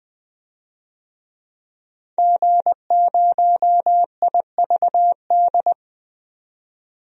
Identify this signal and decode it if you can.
Morse code, heard as a keyed tone, about 21 wpm, 700 Hz. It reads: G0IVD
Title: DXCC-100 Call Signs - 5 Characters - 20wpm